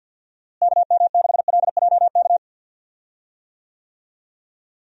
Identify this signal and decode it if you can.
Morse code, heard as a keyed tone, about 49 wpm, 700 Hz. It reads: KM6LJK
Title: USA Call Signs - 6 Characters - Repeated 2x - 50wpm